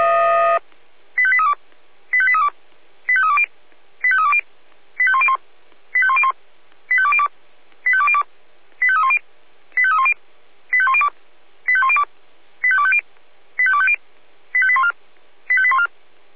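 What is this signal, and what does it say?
Five-tone selective call (ZVEI), heard as quick runs of five tones, tones of about 70 ms. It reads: 85832 85832 85320 85320 85101 85101 85202 85202 85310 85310 85102 85102 85250 85250 85814 85814
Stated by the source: Five-tone ZVEI selective call 85898, received by SDR.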